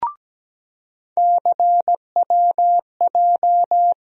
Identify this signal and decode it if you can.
Morse code, heard as a keyed tone, about 17 wpm, 700 Hz. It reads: CWJ
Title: USA Call Sign Postfixes - 3 Letters 17wpm